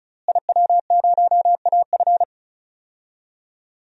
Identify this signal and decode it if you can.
Morse code, heard as a keyed tone, about 36 wpm, 700 Hz. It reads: IW0AF
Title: DXCC-100 Call Signs - 5 Characters - Repeated 2x - 35wpm